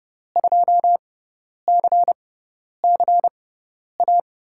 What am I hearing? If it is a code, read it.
Morse code, heard as a keyed tone, about 31 wpm, spaced as for 10 wpm, 700 Hz. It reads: 2CCA
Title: USA Call Signs - 6 Characters - 6x Character Spacing - 30wpm